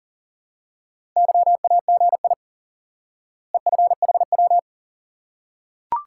Morse code, heard as a keyed tone, about 37 wpm, 700 Hz. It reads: YAGI EFHW
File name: IQE Course 3 - Lesson 89 - YAGI - Two Elements - 40wpm